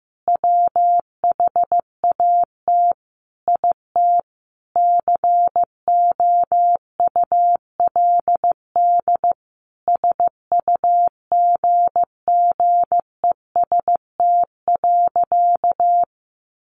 Morse code, heard as a keyed tone, about 15 wpm, 700 Hz. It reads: WHAT IT COULD SUGGEST.
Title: Sentences from Top 1500 Words 15wpm